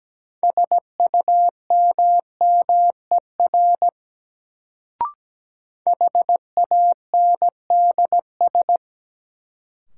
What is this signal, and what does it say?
Morse code, heard as a keyed tone, about 17 wpm, 700 Hz. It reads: SUMMER HANDS
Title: Sentences from Top 1000 Words - Review Part 6 - New Word Drill - 17wpm